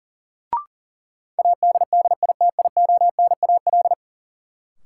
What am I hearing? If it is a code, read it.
Morse code, heard as a keyed tone, about 40 wpm, 700 Hz. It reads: ADDITIONAL
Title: Sentences from Top 1400 Words - New Word Drill Alt - 40wpm